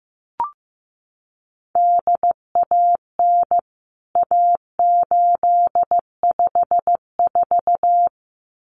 Morse code, heard as a keyed tone, about 15 wpm, 700 Hz. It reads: DAN A854
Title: CWT Exchange 15wpm